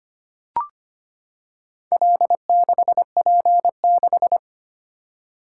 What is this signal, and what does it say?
Morse code, heard as a keyed tone, about 25 wpm, 700 Hz. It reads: L6P6
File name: practice / 25wpm / Four Letter-Number Combinations - 25wpm